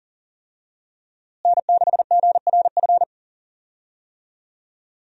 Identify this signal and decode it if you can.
Morse code, heard as a keyed tone, about 40 wpm, 700 Hz. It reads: N6GRF
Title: USA Call Signs - 5 Characters - Repeated 3x - 40wpm